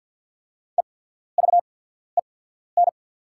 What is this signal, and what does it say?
Morse code, heard as a keyed tone, about 48 wpm, spaced as for 14 wpm, 700 Hz. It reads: EVEN
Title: 8x Character Spacing - Top 300 Words - 50wpm